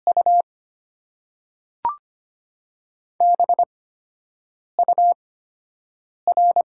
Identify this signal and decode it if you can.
Morse code, heard as a keyed tone, about 25 wpm, spaced as for 7 wpm, 700 Hz. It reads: U BUR
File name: Top 90 QSO Elements - 8x Character Spacing - 25wpm